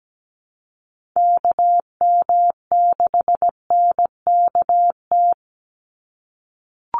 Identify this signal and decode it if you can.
Morse code, heard as a keyed tone, about 17 wpm, 700 Hz. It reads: KM6NKT
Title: USA Call Signs 17wpm